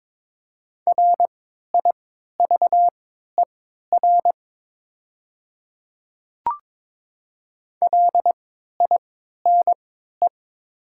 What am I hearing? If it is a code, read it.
Morse code, heard as a keyed tone, about 22 wpm, spaced as for 12 wpm, 700 Hz. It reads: RIVER LINE
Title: 3x Character Spacing - Top 500 Words - 22wpm